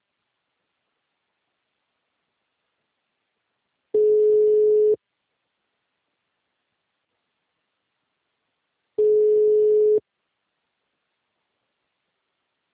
Ringback tone (Europe, 425 Hz).
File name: out- -2007- - - .35389